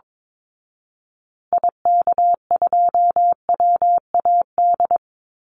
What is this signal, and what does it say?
Morse code, heard as a keyed tone, about 22 wpm, 700 Hz. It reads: IK2WAD
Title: Call Signs Repeated 2x - 22wpm